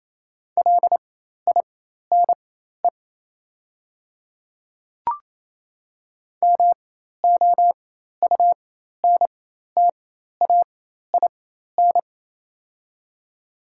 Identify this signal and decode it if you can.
Morse code, heard as a keyed tone, about 28 wpm, spaced as for 13 wpm, 700 Hz. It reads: LINE MOUNTAIN
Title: 4x Character Spacing - Top 500 Words - 28wpm